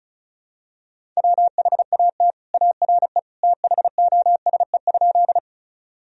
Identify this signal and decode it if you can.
Morse code, heard as a keyed tone, about 34 wpm, 690 Hz. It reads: WHAT ARE THOSE?
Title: Sentences from Top 1000 Words - Review Part 1 - 35wpm